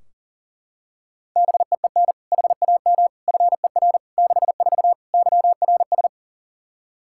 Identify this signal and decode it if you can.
Morse code, heard as a keyed tone, about 37 wpm, 700 Hz. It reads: BEEN HAM FER 64 YRS
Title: K1USN SST QSO Practice 40wpm